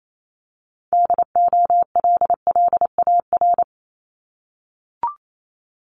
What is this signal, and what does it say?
Morse code, heard as a keyed tone, about 28 wpm, 700 Hz. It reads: DOLLAR
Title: Multi-letter Phonemes 28wpm